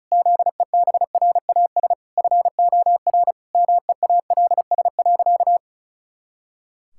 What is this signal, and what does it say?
Morse code, heard as a keyed tone, about 36 wpm, 700 Hz. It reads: ZEBRAS FOR MEALS.
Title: Pangram Challenge v2 - 35wpm